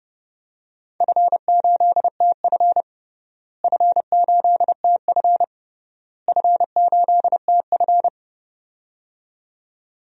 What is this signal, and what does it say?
Morse code, heard as a keyed tone, about 29 wpm, 700 Hz. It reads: F8TF F8TF F8TF
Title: DXCC-100 Call Signs - 4 Characters - Repeated 3x - 30wpm